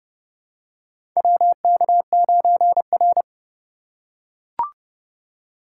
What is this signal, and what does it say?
Morse code, heard as a keyed tone, about 30 wpm, 700 Hz. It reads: WK9R
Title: USA Call Signs - 4 Characters - 4x Character Spacing - 30wpm